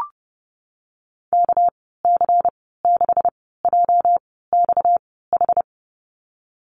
Morse code, heard as a keyed tone, about 30 wpm, spaced as for 17 wpm, 700 Hz. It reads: KC6JXH
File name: Lesson 37g - Letter x - Call Signs 30wpm